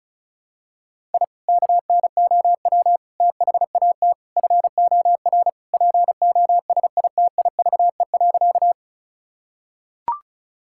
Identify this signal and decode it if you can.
Morse code, heard as a keyed tone, about 36 wpm, 700 Hz. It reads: I KNOW THAT FOR POSITIVE.
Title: Sentences from Top 1000 Words - Review Part 5 - 35wpm